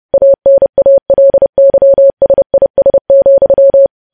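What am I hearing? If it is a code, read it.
Morse code, heard as a keyed tone, about 30 wpm, 550 Hz. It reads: ANALYSIS,
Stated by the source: News headlines sent in Morse at 30.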